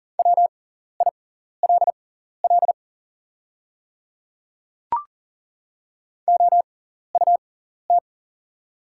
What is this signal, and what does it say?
Morse code, heard as a keyed tone, about 40 wpm, spaced as for 14 wpm, 700 Hz. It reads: WILL OUT
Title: 6x Character Spacing - Top 100 Words - 40wpm